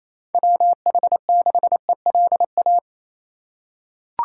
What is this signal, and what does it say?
Morse code, heard as a keyed tone, about 28 wpm, 700 Hz. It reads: WH6ELA
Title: USA Call Signs - 6 Characters - 2x Character Spacing - 28wpm